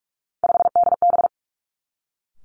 Morse code, heard as a keyed tone, about 46 wpm, 700 Hz. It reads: 5DB
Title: Three Challenging Letter-Number Combinations 45wpm